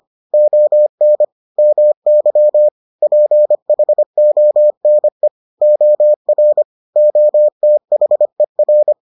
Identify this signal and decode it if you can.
Morse code, heard as a keyed tone, about 25 wpm, 600 Hz. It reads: ON MY PHONE OR OTHER